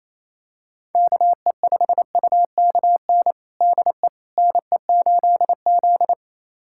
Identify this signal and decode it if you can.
Morse code, heard as a keyed tone, about 28 wpm, 700 Hz. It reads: KE5UKN DE NE8Z
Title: USA Callsign DE Callsign 28wpm